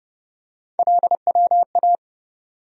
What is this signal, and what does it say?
Morse code, heard as a keyed tone, about 30 wpm, 700 Hz. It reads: LWA
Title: USA Call Sign Postfixes - 3 Letters 30wpm